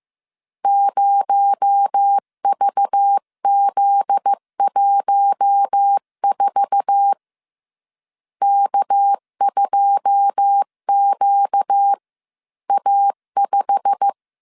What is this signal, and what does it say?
Morse code, heard as a keyed tone, about 15 wpm, 800 Hz. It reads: 0VZ14 K2QA5
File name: CW / DET / 1 9 1500C S B 14wpm 3302